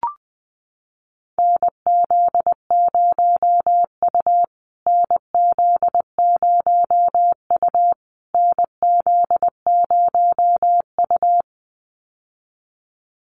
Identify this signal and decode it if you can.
Morse code, heard as a keyed tone, about 20 wpm, 700 Hz. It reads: NZ0U NZ0U NZ0U
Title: USA Call Signs Repeated 3x 20wpm